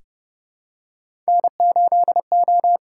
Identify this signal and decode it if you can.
Morse code, heard as a keyed tone, about 30 wpm, 700 Hz. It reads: N8O
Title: USA Call Signs - 4 Characters - 8x Character Spacing - 30wpm